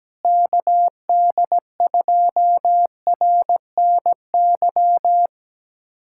Morse code, heard as a keyed tone, about 17 wpm, 700 Hz. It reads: KD2RNY
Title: USA Callsign DE Callsign 17wpm